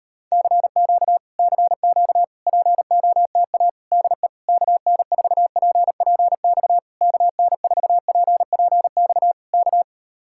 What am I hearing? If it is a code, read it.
Morse code, heard as a keyed tone, about 38 wpm, 700 Hz. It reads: CQ CQ POTA DE KN4PPX KN4PPX K